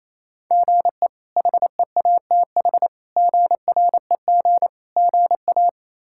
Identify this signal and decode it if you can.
Morse code, heard as a keyed tone, about 28 wpm, 700 Hz. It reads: GE HEATH GREG GA
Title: K1USN SST QSO Practice - 28wpm